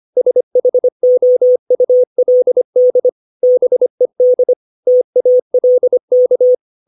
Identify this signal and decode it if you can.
Morse code, heard as a keyed tone, about 25 wpm, 500 Hz. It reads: SHOULD BED TALK
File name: 632 words / 632 Words 25WPM